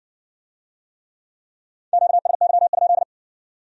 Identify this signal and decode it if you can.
Morse code, heard as a keyed tone, about 56 wpm, 700 Hz. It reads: QSY?